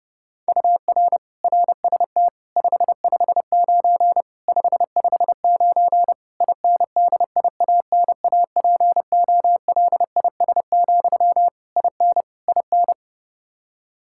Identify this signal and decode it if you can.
Morse code, heard as a keyed tone, about 32 wpm, 700 Hz. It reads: UR RST 559 559 INDIANAPOLIS, IN IN
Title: Practice QSO - Exchange 1 - Beginning - 30wpm